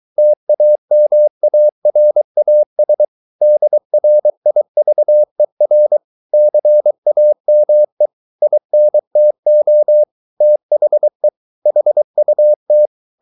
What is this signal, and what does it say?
Morse code, heard as a keyed tone, about 23 wpm, 600 Hz. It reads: TAMARAS DRIVER CAME INTO THE HUT